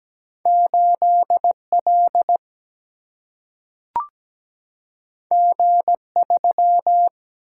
Morse code, heard as a keyed tone, about 17 wpm, 700 Hz. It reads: 8 L G 3
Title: Single-Letter Single-Number 17wpm